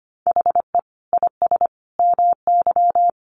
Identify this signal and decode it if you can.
Morse code, heard as a keyed tone, about 25 wpm, 700 Hz. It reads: HE IS MY
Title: Sentences from Top 300 Words 25wpm